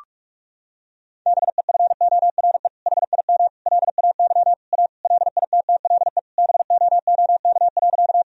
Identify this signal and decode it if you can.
Morse code, heard as a keyed tone, about 41 wpm, 700 Hz. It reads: BEFORE HIM LAY A LITTLE BOOK.
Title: Sentences from Top 1000 Words - Review Part 2 - 45wpm